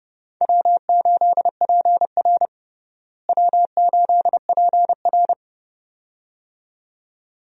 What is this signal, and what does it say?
Morse code, heard as a keyed tone, about 32 wpm, 700 Hz. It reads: W8PR W8PR
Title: USA Call Signs - 4 Characters - Repeated 2x - 30wpm